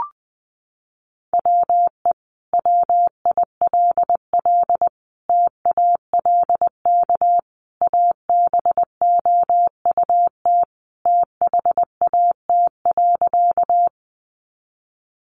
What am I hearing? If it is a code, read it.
Morse code, heard as a keyed tone, about 20 wpm, 700 Hz. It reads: WE WILL TALK ABOUT THAT.
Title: Sentences from Top 400 Words 20wpm